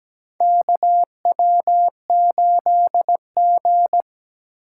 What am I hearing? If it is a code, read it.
Morse code, heard as a keyed tone, about 17 wpm, 700 Hz. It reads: KW8G
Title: USA Call Signs - 4 Characters - 2x Character Spacing - 17wpm